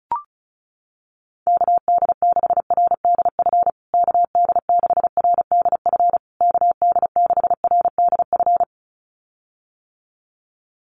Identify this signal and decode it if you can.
Morse code, heard as a keyed tone, about 34 wpm, 700 Hz. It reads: KD6RDF KD6RDF KD6RDF
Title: USA Call Signs Repeated 3x 35wpm